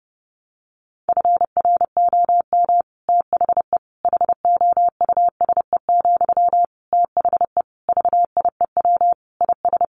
Morse code, heard as a keyed tone, about 30 wpm, 700 Hz. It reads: FROM THE HOUSE, THE VIEW IS
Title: Sentences from Top 1100 Words 30wpm